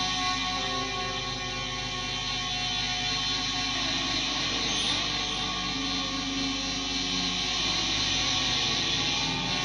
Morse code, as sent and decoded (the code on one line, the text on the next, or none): none